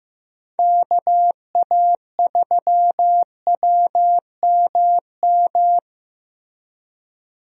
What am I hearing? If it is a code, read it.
Morse code, heard as a keyed tone, about 15 wpm, 700 Hz. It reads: KA3WMM
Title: USA Call Signs - 6 Characters - Repeated 2x - 15wpm